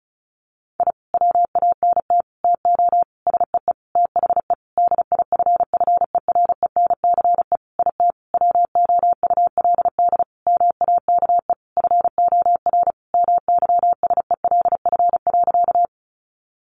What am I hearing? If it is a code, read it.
Morse code, heard as a keyed tone, about 34 wpm, 700 Hz. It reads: I WANT TO SEE THE DIFFERENCE IT WOULD MAKE FOR MYSELF.
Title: Sentences from Top 1300 Words 35wpm